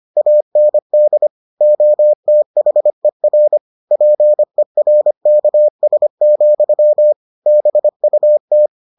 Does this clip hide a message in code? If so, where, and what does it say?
Yes: Morse code from 0.2 to 8.7 s, heard as a keyed tone, about 25 wpm, 600 Hz. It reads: AND OTHER PERKS, BUT